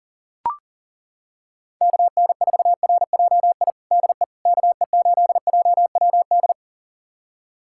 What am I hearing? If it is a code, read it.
Morse code, heard as a keyed tone, about 40 wpm, 700 Hz. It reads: KN4RJI DE KE8JWD
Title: USA Callsign DE Callsign - 6 Characters - 40wpm